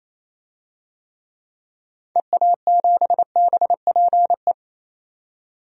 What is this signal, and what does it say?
Morse code, heard as a keyed tone, about 28 wpm, 700 Hz. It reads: EA7BPE
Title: DXCC-100 Call Signs - 6 Characters - 28wpm